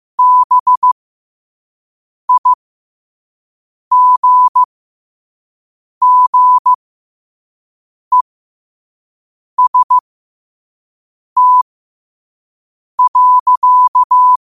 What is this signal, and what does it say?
Morse code, heard as a keyed tone, about 15 wpm, spaced as for 5 wpm, 1000 Hz. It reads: BIGGEST.
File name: qotdfiles / 5wpm